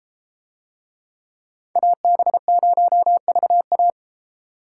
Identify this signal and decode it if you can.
Morse code, heard as a keyed tone, about 31 wpm, 700 Hz. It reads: AB0VA